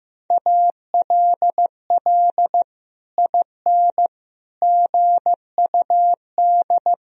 Morse code, heard as a keyed tone, about 16 wpm, 700 Hz. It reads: ALL IN GUD
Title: Sentences from Top 200 Words and Abbreviations 15wpm